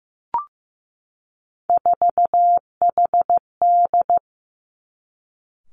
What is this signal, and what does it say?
Morse code, heard as a keyed tone, about 15 wpm, 700 Hz. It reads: 4HD
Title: Three Challenging Letter-Number Combinations 15wpm